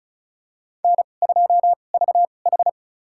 Morse code, heard as a keyed tone, about 36 wpm, spaced as for 25 wpm, 700 Hz. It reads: N2VH
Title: USA Call Signs - 2x Character Spacing - 35wpm